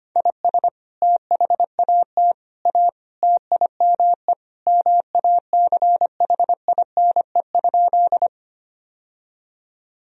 Morse code, heard as a keyed tone, about 25 wpm, 700 Hz. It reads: IS THAT A TIME MACHINE?
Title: Sentences from Top 1000 Words - Review Part 2 - 25wpm